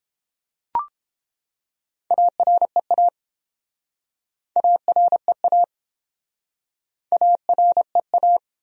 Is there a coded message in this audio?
Yes: Morse code from 2.1 to 8.4 s, heard as a keyed tone, about 30 wpm, 700 Hz. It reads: AREA AREA AREA